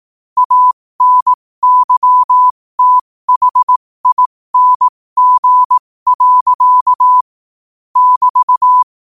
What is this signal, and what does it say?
Morse code, heard as a keyed tone, about 18 wpm, spaced as for 15 wpm, 1000 Hz. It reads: ANYTHING. <BT>
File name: qotdfiles / 15wpm